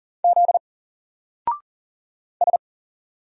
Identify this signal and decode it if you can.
Morse code, heard as a keyed tone, about 40 wpm, 700 Hz. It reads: Z S
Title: Single Character (Letter:Number:Punctuation:Prosign) 40wpm